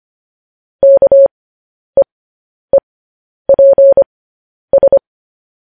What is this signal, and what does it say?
Morse code, heard as a keyed tone, about 25 wpm, spaced as for 10 wpm, 550 Hz. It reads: KEEPS
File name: podcast / News headlines-1x-25x10